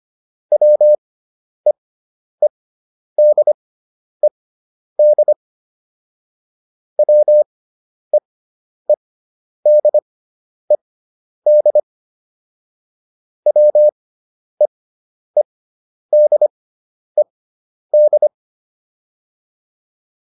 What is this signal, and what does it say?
Morse code, heard as a keyed tone, about 25 wpm, spaced as for 10 wpm, 600 Hz. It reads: WEEDED WEEDED WEEDED